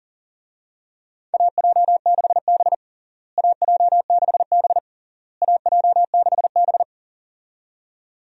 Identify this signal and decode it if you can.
Morse code, heard as a keyed tone, about 40 wpm, 700 Hz. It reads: AJ6B AJ6B AJ6B
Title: USA Call Signs - 4 Characters - Repeated 3x - 40wpm